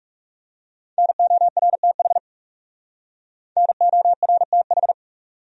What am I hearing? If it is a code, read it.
Morse code, heard as a keyed tone, about 38 wpm, 700 Hz. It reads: NORTH NORTH